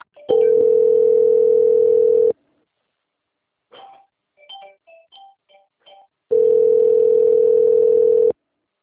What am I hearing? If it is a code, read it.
Ringback tone (North America), 440 + 480 Hz.